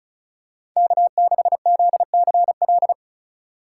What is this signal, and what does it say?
Morse code, heard as a keyed tone, about 36 wpm, 700 Hz. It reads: K6ZCL
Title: K1USN SST QSO Practice - 35wpm